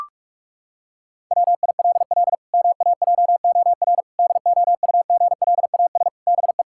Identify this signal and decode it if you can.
Morse code, heard as a keyed tone, about 42 wpm, 700 Hz. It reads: WILL MAJOR DOUGLAS BE
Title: Pangram Challenge v2 - 45wpm